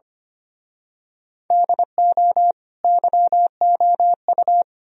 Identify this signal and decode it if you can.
Morse code, heard as a keyed tone, about 29 wpm, 700 Hz. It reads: DO YOU
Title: Speed-Racing - Sets of 4 Words 25wpm